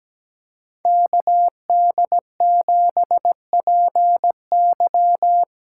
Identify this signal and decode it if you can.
Morse code, heard as a keyed tone, about 17 wpm, 700 Hz. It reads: KD7PY
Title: USA Callsign DE Callsign - 5 Characters - 17wpm